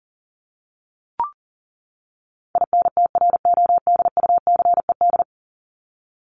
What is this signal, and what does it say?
Morse code, heard as a keyed tone, about 37 wpm, 700 Hz. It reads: INTRODUCED
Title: ED Suffix - Top 100 Words with Base Words 40wpm